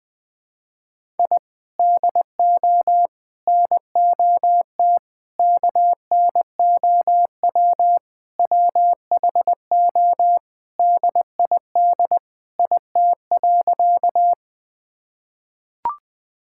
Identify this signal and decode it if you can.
Morse code, heard as a keyed tone, about 20 wpm, 700 Hz. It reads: I DO NOT KNOW WHO DID IT.
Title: Sentences with 5-Word Phrases - 20wpm